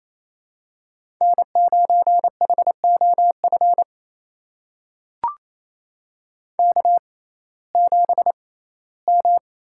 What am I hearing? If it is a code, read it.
Morse code, heard as a keyed tone, about 28 wpm, spaced as for 17 wpm, 700 Hz. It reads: N9HOF K7M
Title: USA Call Signs - 6x Character Spacing - 28wpm